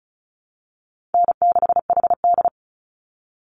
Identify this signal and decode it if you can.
Morse code, heard as a keyed tone, about 34 wpm, 700 Hz. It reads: N6HD
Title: Easy Call Signs Repeated 2x - 35wpm